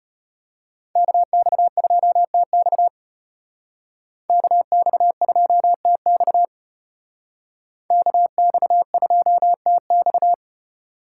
Morse code, heard as a keyed tone, about 33 wpm, 700 Hz. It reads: KX2TX KX2TX KX2TX